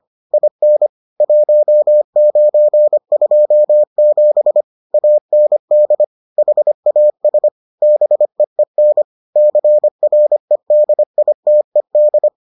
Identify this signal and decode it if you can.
Morse code, heard as a keyed tone, about 25 wpm, 600 Hz. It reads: IN 1927 AND HAS BEEN CREDITED